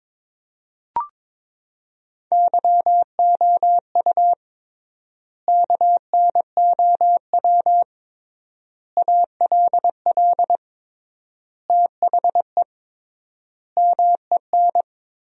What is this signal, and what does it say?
Morse code, heard as a keyed tone, about 22 wpm, 700 Hz. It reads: YOU KNOW ALL THE MEN
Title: 3x Word-Spacing - Sentences from Top 300 Words - 22wpm